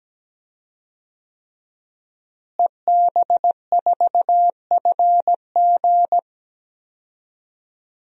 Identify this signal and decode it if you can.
Morse code, heard as a keyed tone, about 17 wpm, 700 Hz. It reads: EB4FG
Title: DXCC-100 Call Signs - 5 Characters - Repeated 3x - 17wpm